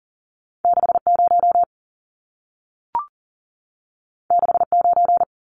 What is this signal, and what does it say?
Morse code, heard as a keyed tone, about 40 wpm, 700 Hz. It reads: 60 69
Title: Numbers - 2-Digit 40wpm